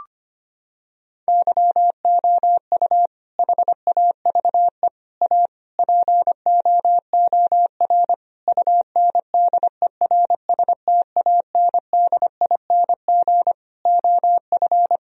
Morse code, heard as a keyed tone, about 25 wpm, 700 Hz. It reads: YOU HAVE A POOR UNDERSTANDING OF
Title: Sentences from Top 1000 Words - Review Part 3 - 25wpm